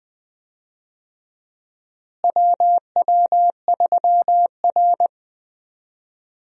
Morse code, heard as a keyed tone, about 21 wpm, 700 Hz. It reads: WW3R
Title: USA Call Signs - 4 Characters - Repeated 2x - 20wpm